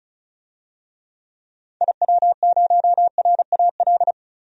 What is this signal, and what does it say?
Morse code, heard as a keyed tone, about 34 wpm, 700 Hz. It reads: IW0RAL
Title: DXCC-100 Call Signs - 6 Characters - 35wpm